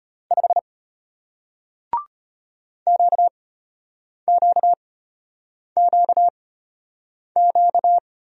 Morse code, heard as a keyed tone, about 34 wpm, spaced as for 8 wpm, 700 Hz. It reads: 5 QQQQ